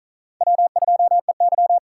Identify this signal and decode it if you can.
Morse code, heard as a keyed tone, about 48 wpm, 700 Hz. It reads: W2EY